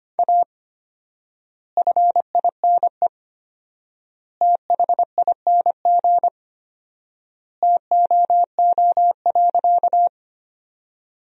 Morse code, heard as a keyed tone, about 25 wpm, 700 Hz. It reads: A FINE THING TOO.
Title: 4x Word-Spacing - Sentences from Top 500 Words - 25wpm